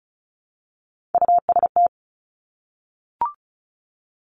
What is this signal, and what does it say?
Morse code, heard as a keyed tone, about 36 wpm, 700 Hz. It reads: UST
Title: Top Three Letter Combinations in Words 35wpm